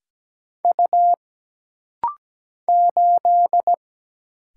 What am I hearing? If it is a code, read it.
Morse code, heard as a keyed tone, about 17 wpm, 700 Hz. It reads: U 8
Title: Single Letter-Number - Rapid-Fire 17wpm